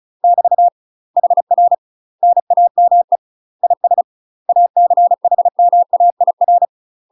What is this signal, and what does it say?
Morse code, heard as a keyed tone, about 35 wpm, 700 Hz. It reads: <BT> HR NAME IS ACHMAIR
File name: UK MASTER QSOs TRAINING / 35WPM / UK MASTER QSOs 0042